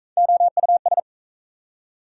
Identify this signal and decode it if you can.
Morse code, heard as a keyed tone, about 44 wpm, 700 Hz. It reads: OUS